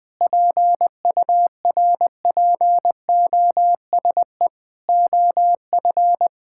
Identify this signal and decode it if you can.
Morse code, heard as a keyed tone, about 19 wpm, 700 Hz. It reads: PURPOSE OF
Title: Sentences with 5-Word Phrases - 20wpm